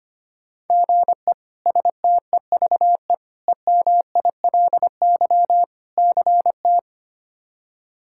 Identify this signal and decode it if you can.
Morse code, heard as a keyed tone, about 25 wpm, 700 Hz. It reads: GE STEVE EMILY CT
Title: K1USN SST QSO Practice - 25wpm